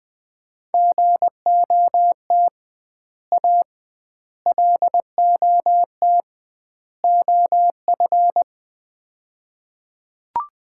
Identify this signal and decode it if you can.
Morse code, heard as a keyed tone, about 21 wpm, 700 Hz. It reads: GOT A LOT OF
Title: 2x Word-Spacing - Sets of 4 Words - 20wpm